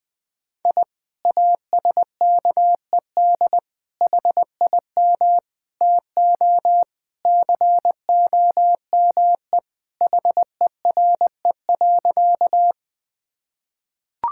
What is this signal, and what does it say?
Morse code, heard as a keyed tone, about 21 wpm, 700 Hz. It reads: I ASKED HIM TO COME HERE.
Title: Sentences with 2-Word Phrases - 20wpm